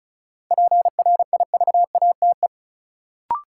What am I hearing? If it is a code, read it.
Morse code, heard as a keyed tone, about 36 wpm, 700 Hz. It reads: PRIVATE
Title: Sentences from Top 1000 Words - New Word Drill - 35wpm